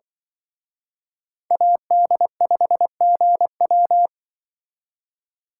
Morse code, heard as a keyed tone, about 24 wpm, 700 Hz. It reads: AD5GW
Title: Speed Racing - USA 2x2 Call Signs 20wpm